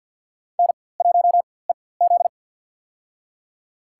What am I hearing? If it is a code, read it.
Morse code, heard as a keyed tone, about 52 wpm, spaced as for 23 wpm, 700 Hz. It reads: N1EZ
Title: USA Call Signs - 4x Character Spacing - 50wpm